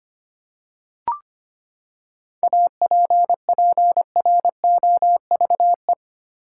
Morse code, heard as a keyed tone, about 25 wpm, 700 Hz. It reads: APPROVE
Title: Seven Letter Words - 25wpm